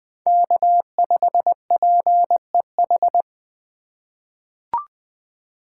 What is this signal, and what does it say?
Morse code, heard as a keyed tone, about 21 wpm, 700 Hz. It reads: K5PEH
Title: USA Call Signs - 5 Characters - 3x Character Spacing - 20wpm